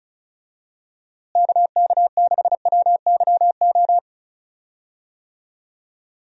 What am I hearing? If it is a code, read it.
Morse code, heard as a keyed tone, about 34 wpm, 700 Hz. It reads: KK6WYO
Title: USA Call Signs - 6 Characters - Repeated 2x - 35wpm